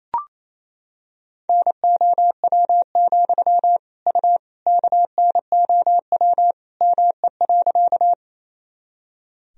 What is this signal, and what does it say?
Morse code, heard as a keyed tone, about 27 wpm, 700 Hz. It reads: NOW, U KNOW ME.
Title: Sentences from Top 200 Words and Abbreviations 28wpm